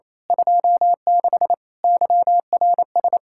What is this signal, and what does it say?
Morse code, heard as a keyed tone, about 27 wpm, 700 Hz. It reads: 26 YRS
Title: Speed-Racing Common QSO Phrases - 28wpm